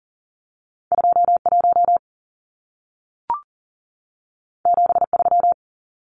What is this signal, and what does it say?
Morse code, heard as a keyed tone, about 40 wpm, 700 Hz. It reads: 21 73
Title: Numbers - 2-Digit 40wpm